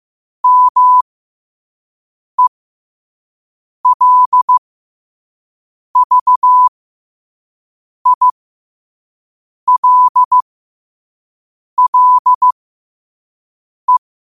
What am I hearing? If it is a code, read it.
Morse code, heard as a keyed tone, about 15 wpm, spaced as for 5 wpm, 1000 Hz. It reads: MELVILLE
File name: qotdfiles / 5wpm